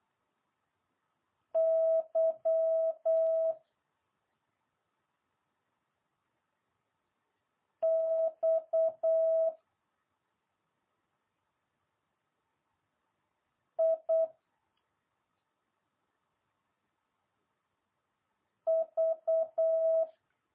Morse code, heard as a keyed tone, about 8 wpm, 650 Hz. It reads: Y X I V